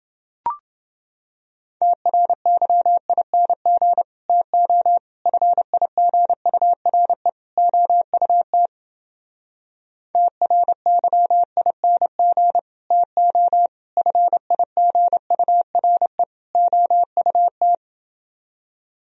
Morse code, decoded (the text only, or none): TRYING TO FIGURE OUT TRYING TO FIGURE OUT